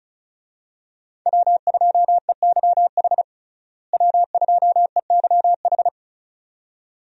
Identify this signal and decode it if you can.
Morse code, heard as a keyed tone, about 36 wpm, 700 Hz. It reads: W2EYH W2EYH
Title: USA Call Signs - 5 Characters - Repeated 2x - 35wpm